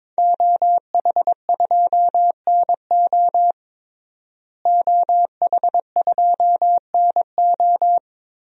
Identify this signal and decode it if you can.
Morse code, heard as a keyed tone, about 22 wpm, 700 Hz. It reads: OH2NO OH2NO
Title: DXCC-100 Call Signs - 5 Characters - Repeated 2x - 22wpm